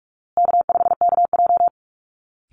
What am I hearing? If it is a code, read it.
Morse code, heard as a keyed tone, about 44 wpm, 700 Hz. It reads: K5KJ
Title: Call Signs Repeated 2x - 45wpm